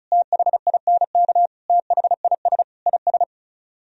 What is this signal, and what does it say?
Morse code, heard as a keyed tone, about 44 wpm, 700 Hz. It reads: THINK THIS IS